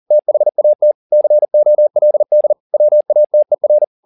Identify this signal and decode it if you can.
Morse code, heard as a keyed tone, about 40 wpm, 600 Hz. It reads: THAT COLD WATER